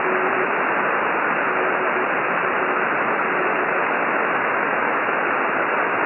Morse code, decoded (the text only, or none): none